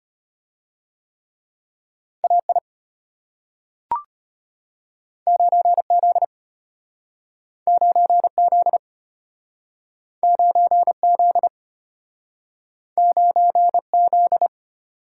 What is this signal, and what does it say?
Morse code, heard as a keyed tone, about 33 wpm, 700 Hz. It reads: AI 9Z 9Z 9Z 9Z